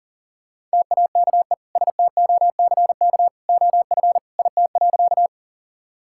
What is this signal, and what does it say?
Morse code, heard as a keyed tone, about 37 wpm, 700 Hz. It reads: TAKE STOCK OF IT.
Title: Sentences from Top 1000 Words - Review Part 5 - 40wpm